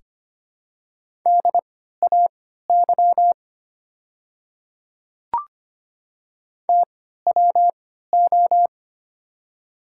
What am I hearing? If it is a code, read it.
Morse code, heard as a keyed tone, about 25 wpm, spaced as for 14 wpm, 700 Hz. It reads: DAY TWO
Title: 3x Character Spacing - Top 200 Words - 25wpm